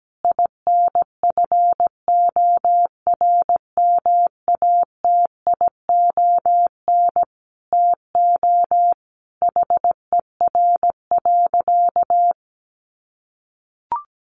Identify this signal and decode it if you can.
Morse code, heard as a keyed tone, about 17 wpm, 700 Hz. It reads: INFORMATION TO HER.
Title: Sentences from Top 600 Words 17wpm